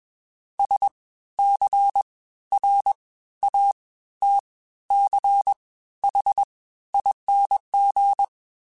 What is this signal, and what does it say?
Morse code, heard as a keyed tone, about 20 wpm, spaced as for 13 wpm, 800 Hz. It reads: SCRATCHING